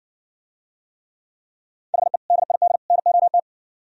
Morse code, heard as a keyed tone, about 59 wpm, 700 Hz. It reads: HE DID NOT